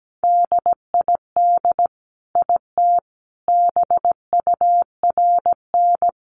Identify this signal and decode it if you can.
Morse code, heard as a keyed tone, about 17 wpm, 700 Hz. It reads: DID IT BURN
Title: Sentences from Top 1300 Words 17wpm